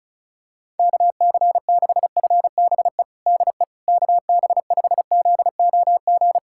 Morse code, heard as a keyed tone, about 36 wpm, 700 Hz. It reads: KC6FBE DE KB5ZOG
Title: USA Callsign DE Callsign - 6 Characters - 35wpm